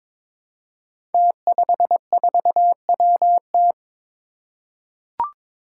Morse code, heard as a keyed tone, about 22 wpm, 700 Hz. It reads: T54WT
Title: Five Challenging Letter-Number Combinations 22wpm